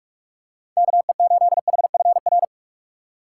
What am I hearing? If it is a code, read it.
Morse code, heard as a keyed tone, about 46 wpm, 700 Hz. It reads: KE8HFR
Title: USA Call Signs - 4x Character Spacing - 45wpm